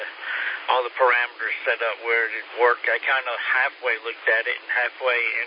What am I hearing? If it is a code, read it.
Emergency Alert System (SAME) tones, heard as harsh buzzing bursts, with speech. It is heard from 1.2 to 4.9 s.